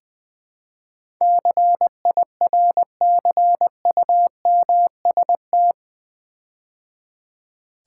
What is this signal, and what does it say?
Morse code, heard as a keyed tone, about 20 wpm, 700 Hz. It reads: CIRCUMST
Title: Build A Word - 4 Letters At A Time - 12-Letter Words - 20wpm